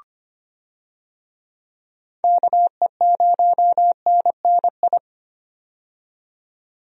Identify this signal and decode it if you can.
Morse code, heard as a keyed tone, about 25 wpm, 700 Hz. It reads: KE0NNI
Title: USA Call Signs - 6 Characters - 25wpm